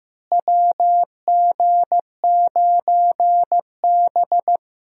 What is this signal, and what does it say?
Morse code, heard as a keyed tone, about 15 wpm, 700 Hz. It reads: WG9B
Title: USA Call Signs - 4 Characters - Repeated 2x - 15wpm